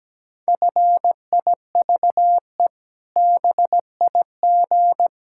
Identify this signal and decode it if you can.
Morse code, heard as a keyed tone, about 17 wpm, 700 Hz. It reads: FIVE BIG
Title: Pangram Challenge v2 - 17wpm